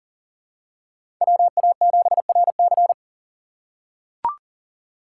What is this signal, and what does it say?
Morse code, heard as a keyed tone, about 40 wpm, 700 Hz. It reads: WA7RC
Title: USA Call Signs - 5 Characters - 2x Character Spacing - 40wpm